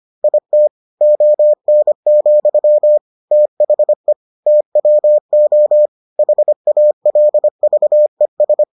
Morse code, heard as a keyed tone, about 25 wpm, 600 Hz. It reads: IT ON, THE TWO HALVES